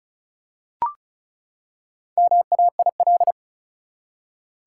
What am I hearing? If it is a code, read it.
Morse code, heard as a keyed tone, about 34 wpm, 700 Hz. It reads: MAIL
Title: Four Letter Words - 35wpm